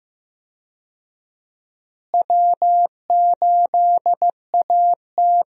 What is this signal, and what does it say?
Morse code, heard as a keyed tone, about 15 wpm, 700 Hz. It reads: W8AT
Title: USA Call Signs - 4 Characters - Repeated 2x - 15wpm